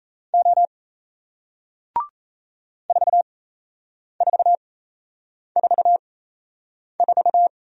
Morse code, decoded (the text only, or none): O 4444